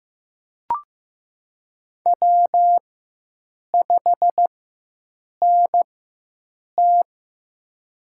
Morse code, heard as a keyed tone, about 15 wpm, spaced as for 7 wpm, 700 Hz. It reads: W5NT
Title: USA Call Signs - 4 Characters - 4x Character Spacing - 15wpm